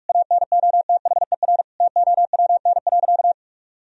Morse code, heard as a keyed tone, about 41 wpm, 700 Hz. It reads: ANOTHER TOWN.
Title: Sentences from Top 1000 Words - Review Part 2 - 45wpm